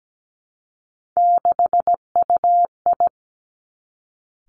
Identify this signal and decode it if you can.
Morse code, heard as a keyed tone, about 17 wpm, 700 Hz. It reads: 6UI
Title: Three Challenging Letter-Number Combinations 17wpm